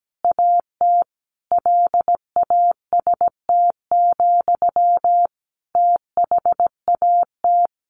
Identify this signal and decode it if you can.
Morse code, heard as a keyed tone, about 17 wpm, 700 Hz. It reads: AT LAST, THAT
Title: Sentences from Top 1500 Words 17wpm